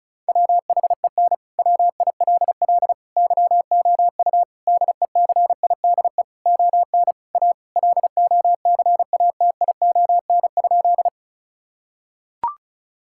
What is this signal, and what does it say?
Morse code, heard as a keyed tone, about 36 wpm, 700 Hz. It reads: WHEN WILL YOU DECIDE ON A LOCATION?
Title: Sentences from Top 1000 Words - Review Part 6 - 35wpm